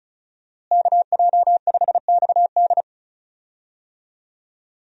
Morse code, heard as a keyed tone, about 36 wpm, 700 Hz. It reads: KJ5XD
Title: USA Call Signs - 5 Characters - Repeated 2x - 35wpm